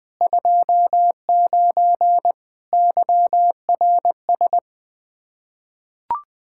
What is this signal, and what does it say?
Morse code, heard as a keyed tone, about 20 wpm, 700 Hz. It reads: 29 YRS
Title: K1USN SST QSO Practice 20wpm